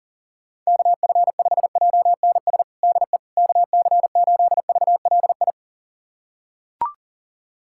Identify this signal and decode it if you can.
Morse code, heard as a keyed tone, about 40 wpm, 700 Hz. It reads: KF5JNS DE KC8VLI
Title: USA Callsign DE Callsign - 6 Characters - 40wpm